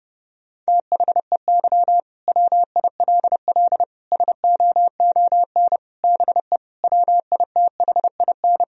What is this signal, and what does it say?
Morse code, heard as a keyed tone, about 30 wpm, 700 Hz. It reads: THEY WILL SOON BE WITHIN
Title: Sentences from Top 1000 Words - Review Part 4 - 30wpm